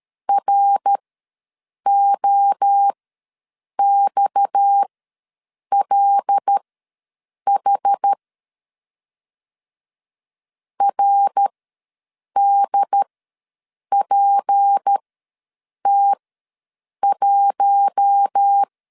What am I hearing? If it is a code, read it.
Morse code, heard as a keyed tone, about 13 wpm, spaced as for 7 wpm, 800 Hz. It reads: ROXLH RDPT1
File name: LEZIONI / 6 PKX0 / 6 PKX0 ws 12wpm 3448